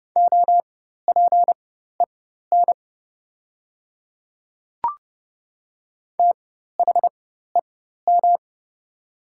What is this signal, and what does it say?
Morse code, heard as a keyed tone, about 30 wpm, spaced as for 14 wpm, 700 Hz. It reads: OPEN THEM
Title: 4x Character Spacing - Top 300 Words - 30wpm